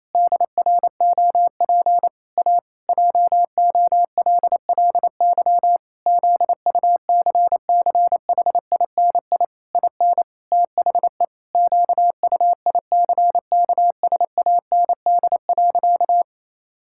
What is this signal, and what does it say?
Morse code, heard as a keyed tone, about 28 wpm, 700 Hz. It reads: DROP A JOLLY ZUCCHINI IN THE QUICKSAND.
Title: Pangram Challenge v2 - 28wpm